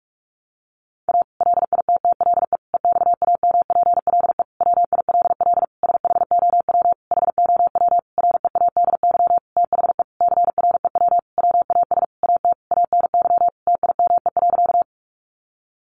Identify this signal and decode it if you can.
Morse code, heard as a keyed tone, about 44 wpm, 700 Hz. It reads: A LITTLE EXAMPLE WILL SHOW HOW READY THE CREW WAS AT ANY TIME.
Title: Sentences from Top 1500 Words 45wpm